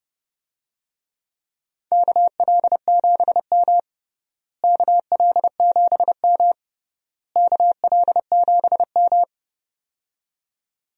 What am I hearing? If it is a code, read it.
Morse code, heard as a keyed tone, about 30 wpm, 700 Hz. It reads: KL7M KL7M KL7M
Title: DXCC-100 Call Signs - 4 Characters - Repeated 3x - 30wpm